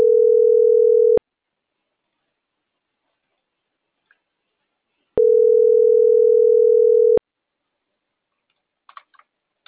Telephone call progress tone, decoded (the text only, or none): ringback tone